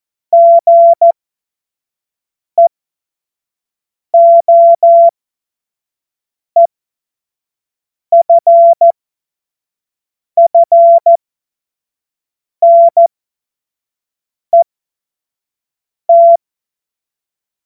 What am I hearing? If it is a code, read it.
Morse code, heard as a keyed tone, about 14 wpm, spaced as for 5 wpm, 680 Hz. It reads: GEOEFFNET
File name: Witz-21-Oktober-2025-25-F70